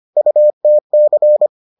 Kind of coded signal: Morse code

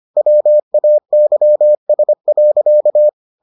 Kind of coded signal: Morse code

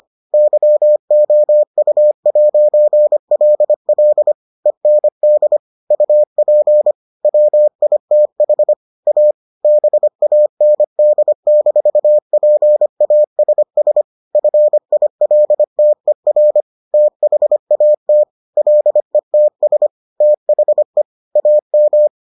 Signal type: Morse code

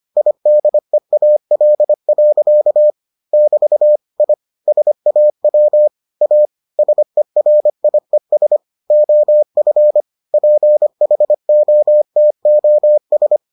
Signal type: Morse code

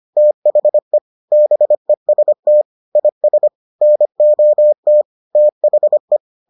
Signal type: Morse code

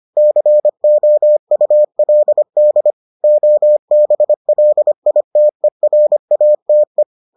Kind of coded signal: Morse code

